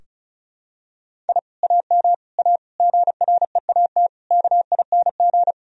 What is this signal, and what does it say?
Morse code, heard as a keyed tone, about 36 wpm, 700 Hz. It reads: I AM A GREAT KING
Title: Sentences from Top 1000 Words - Review Part 4 - 35wpm